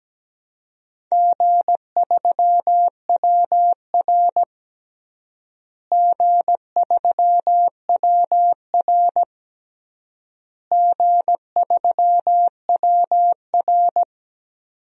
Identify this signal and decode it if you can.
Morse code, heard as a keyed tone, about 17 wpm, 700 Hz. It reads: G3WR G3WR G3WR
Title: DXCC-100 Call Signs - 4 Characters - Repeated 3x - 17wpm